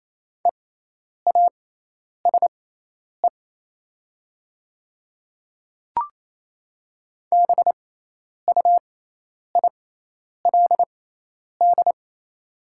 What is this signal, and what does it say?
Morse code, heard as a keyed tone, about 28 wpm, spaced as for 10 wpm, 700 Hz. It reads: EASE BUILD
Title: 6x Character Spacing - Top 300 Words - 28wpm